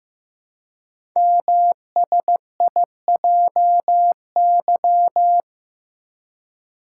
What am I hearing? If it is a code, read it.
Morse code, heard as a keyed tone, about 15 wpm, 700 Hz. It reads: MSIJY
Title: Code Groups - 5 Letters - 15wpm